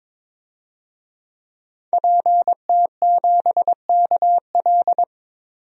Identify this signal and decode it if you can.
Morse code, heard as a keyed tone, about 22 wpm, 700 Hz. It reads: PT7KL
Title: DXCC-100 Call Signs - 5 Characters - 22wpm